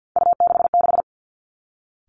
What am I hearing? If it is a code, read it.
Morse code, heard as a keyed tone, about 50 wpm, 700 Hz. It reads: U66
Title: Three Challenging Letter-Number Combinations 50wpm